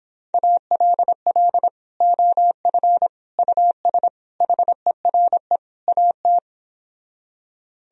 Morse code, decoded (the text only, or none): ALL OF US HERE AT